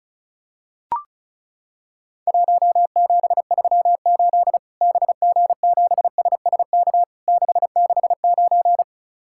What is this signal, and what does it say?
Morse code, heard as a keyed tone, about 34 wpm, 700 Hz. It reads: 1738 BG7SSK 669
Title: Worked All Europe DX Contest - QTC Exchange - 35wpm